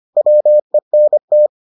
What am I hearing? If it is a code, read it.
Morse code, heard as a keyed tone, about 25 wpm, 600 Hz. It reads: WENT